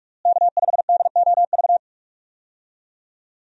Morse code, heard as a keyed tone, about 48 wpm, 700 Hz. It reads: K5DOV
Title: USA Call Signs - 5 Characters - Repeated 3x - 45wpm